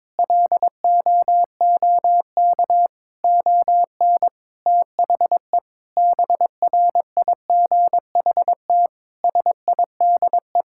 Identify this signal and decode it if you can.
Morse code, heard as a keyed tone, about 21 wpm, 700 Hz. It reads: LOOK ON THE BRIGHT SIDE
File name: practice / 22wpm / Sets of 8 Words - 22wpm